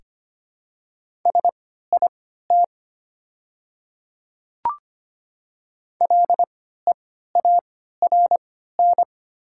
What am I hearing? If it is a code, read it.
Morse code, heard as a keyed tone, about 25 wpm, spaced as for 14 wpm, 690 Hz. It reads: SIT LEARN
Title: 3x Character Spacing - Top 500 Words - 25wpm